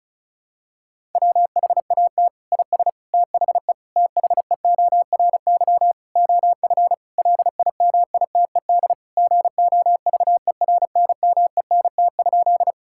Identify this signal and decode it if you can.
Morse code, heard as a keyed tone, about 34 wpm, 700 Hz. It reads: WHAT IS THE THEORY OF LIMITED GOVERNMENT?
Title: Sentences from Top 1000 Words - Review Part 6 - 35wpm